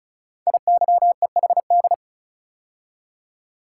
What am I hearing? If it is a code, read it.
Morse code, heard as a keyed tone, about 36 wpm, 700 Hz. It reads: IYEHD
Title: Code Groups - 5 Letters - 35wpm